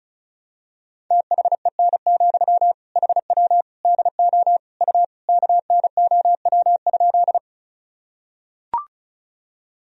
Morse code, heard as a keyed tone, about 34 wpm, 700 Hz. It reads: THEN, HW DO U KNOW?
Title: Sentences from Top 100 Words and Abbreviations 35wpm